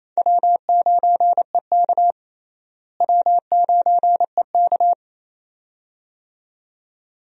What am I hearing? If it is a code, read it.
Morse code, heard as a keyed tone, about 28 wpm, 700 Hz. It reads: W9EK W9EK
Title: USA Call Signs - 4 Characters - Repeated 2x - 28wpm